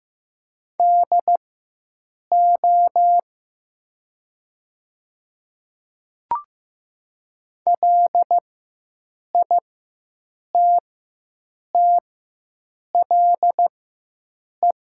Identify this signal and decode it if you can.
Morse code, heard as a keyed tone, about 15 wpm, spaced as for 7 wpm, 700 Hz. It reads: DO LITTLE
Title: 4x Character Spacing - Top 300 Words - 15wpm